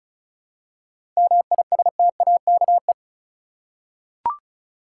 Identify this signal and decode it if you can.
Morse code, heard as a keyed tone, about 36 wpm, 700 Hz. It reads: MISTAKE
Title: Seven Letter Words - 35wpm